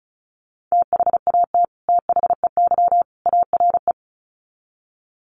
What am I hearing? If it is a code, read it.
Morse code, heard as a keyed tone, about 34 wpm, 700 Hz. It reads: THAT THEY ARE
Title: Sets of 3 Words 35wpm